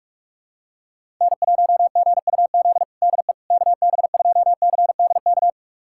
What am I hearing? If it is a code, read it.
Morse code, heard as a keyed tone, about 46 wpm, 700 Hz. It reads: N1GUZ DE KB2CDK
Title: USA Callsign DE Callsign 45wpm